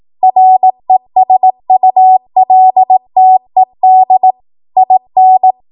Morse code, heard as a keyed tone, about 18 wpm, 750 Hz. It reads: RESULTED IN
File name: Morse / 18WPM